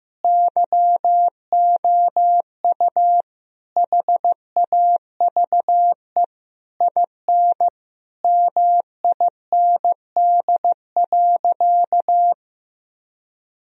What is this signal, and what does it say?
Morse code, heard as a keyed tone, about 15 wpm, 700 Hz. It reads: YOU HAVE IN MIND.
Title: Sentences from Top 1000 Words - Review Part 3 - 15wpm